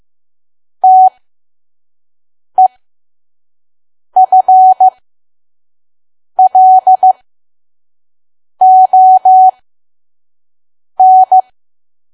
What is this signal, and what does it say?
Morse code, heard as a keyed tone, about 15 wpm, spaced as for 5 wpm, 750 Hz. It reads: TEFLON
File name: Archive / 5 WPM files / PM